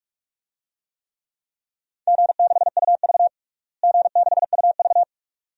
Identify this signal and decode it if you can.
Morse code, heard as a keyed tone, about 46 wpm, 700 Hz. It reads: G6UV G6UV
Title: DXCC-100 Call Signs - 4 Characters - Repeated 2x - 45wpm